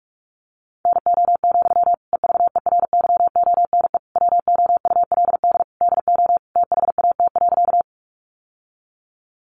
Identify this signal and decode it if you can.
Morse code, heard as a keyed tone, about 44 wpm, 700 Hz. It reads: NO, EVERYONE WOULD DO THAT.
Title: Sentences from Top 800 Words 45wpm